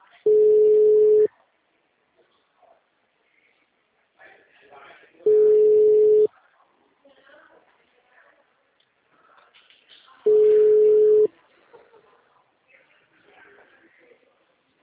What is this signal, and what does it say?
Ringback tone (Europe, 425 Hz), with speech.